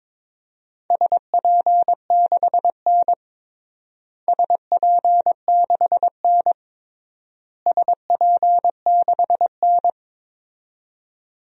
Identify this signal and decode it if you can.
Morse code, heard as a keyed tone, about 22 wpm, 700 Hz. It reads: SP6N SP6N SP6N
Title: DXCC-100 Call Signs - 4 Characters - Repeated 3x - 22wpm